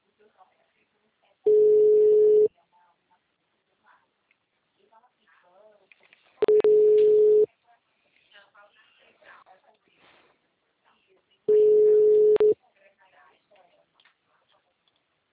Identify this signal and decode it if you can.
Ringback tone (Europe, 425 Hz), with speech.